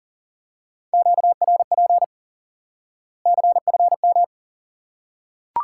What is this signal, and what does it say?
Morse code, heard as a keyed tone, about 40 wpm, 700 Hz. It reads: QRP CFM
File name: course / instant-qso-element-course-3 / IQE Course 3 - Lesson 89 - YAGI - Two Elements - 40wpm